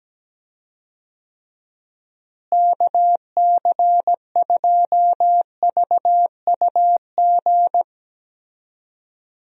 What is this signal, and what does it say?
Morse code, heard as a keyed tone, about 17 wpm, 700 Hz. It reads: KC2VUG
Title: USA Call Signs - 6 Characters - Repeated 3x - 17wpm